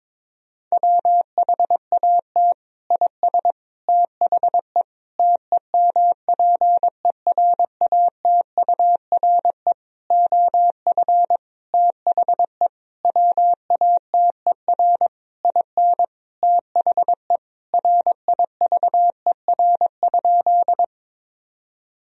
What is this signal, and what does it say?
Morse code, heard as a keyed tone, about 22 wpm, 700 Hz. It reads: WHAT IS THE TEMPERATURE OF THE WATER IN THE RIVER?
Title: Sentences from Top 1000 Words - Review Part 4 - 22wpm